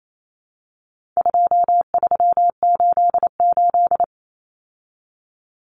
Morse code, heard as a keyed tone, about 28 wpm, 700 Hz. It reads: 2388
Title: Numbers - 4-Digit 28wpm